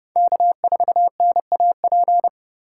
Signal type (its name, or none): Morse code